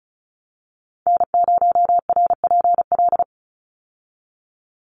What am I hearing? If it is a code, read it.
Morse code, heard as a keyed tone, about 34 wpm, 700 Hz. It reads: N0RPL
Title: USA Call Signs 35wpm